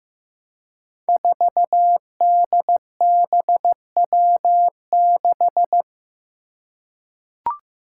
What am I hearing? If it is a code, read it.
Morse code, heard as a keyed tone, about 15 wpm, 700 Hz. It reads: 4DBW6
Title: Five Challenging Letter-Number Combinations 15wpm